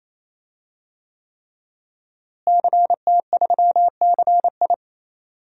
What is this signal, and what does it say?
Morse code, heard as a keyed tone, about 28 wpm, 700 Hz. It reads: CT3CI
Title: DXCC-100 Call Signs - 5 Characters - Repeated 3x - 28wpm